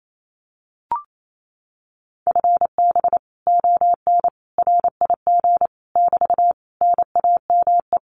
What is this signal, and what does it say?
Morse code, heard as a keyed tone, about 28 wpm, 700 Hz. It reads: FB ON RIG <BT> NAME
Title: Common QSO Phrases Encore 28wpm